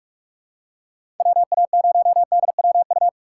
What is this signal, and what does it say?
Morse code, heard as a keyed tone, about 42 wpm, 700 Hz. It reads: WA0DWU
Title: USA Call Signs - 6 Characters - 6x Character Spacing - 45wpm